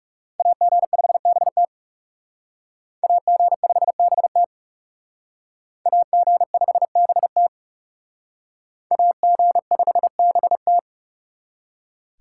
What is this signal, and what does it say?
Morse code, heard as a keyed tone, about 38 wpm, 700 Hz. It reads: AG5BT AG5BT AG5BT AG5BT
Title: Speed Racing - USA Call Signs 30wpm